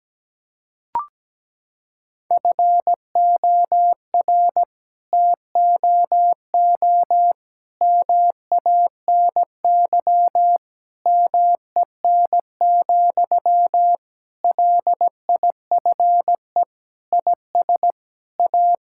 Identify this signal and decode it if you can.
Morse code, heard as a keyed tone, about 17 wpm, 700 Hz. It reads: FOR TOO MANY MEN, LIFE IS A
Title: Sentences from Top 1000 Words - Review Part 6 - 17wpm